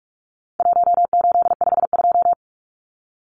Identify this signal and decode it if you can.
Morse code, heard as a keyed tone, about 44 wpm, 700 Hz. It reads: 1852
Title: Numbers - 4-Digit 45wpm